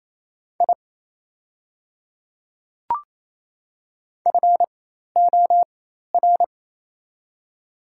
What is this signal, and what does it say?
Morse code, heard as a keyed tone, about 28 wpm, spaced as for 5 wpm, 700 Hz. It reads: IFOR
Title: 4x Character Spacing - Top 100 Words - 28wpm